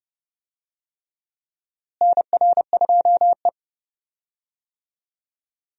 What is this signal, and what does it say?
Morse code, heard as a keyed tone, about 32 wpm, 700 Hz. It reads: NR2E
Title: USA Call Signs - 4 Characters - Repeated 3x - 30wpm